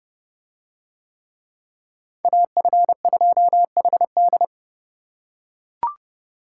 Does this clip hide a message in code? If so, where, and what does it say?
Yes: Morse code from 2.2 to 4.4 s, heard as a keyed tone, about 28 wpm, 700 Hz. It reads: AF2HD